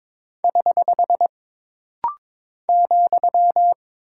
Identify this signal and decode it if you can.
Morse code, heard as a keyed tone, about 22 wpm, 700 Hz. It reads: <HH> ,
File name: practice / 22wpm / Single Punctuation-Prosign - Rapid-Fire - 22wpm